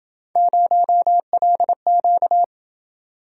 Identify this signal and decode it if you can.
Morse code, heard as a keyed tone, about 29 wpm, 700 Hz. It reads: 0LQ